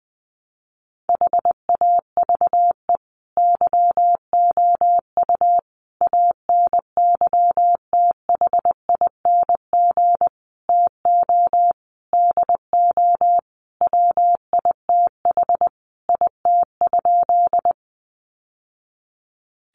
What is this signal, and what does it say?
Morse code, heard as a keyed tone, about 20 wpm, 700 Hz. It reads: HAVE YOU ANYTHING TO DO WITH IT?
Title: Sentences from Top 1000 Words 20wpm